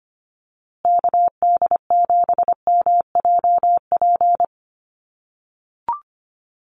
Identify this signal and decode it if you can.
Morse code, heard as a keyed tone, about 24 wpm, 700 Hz. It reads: KD7MJP
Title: USA Call Signs Repeated 3x 25wpm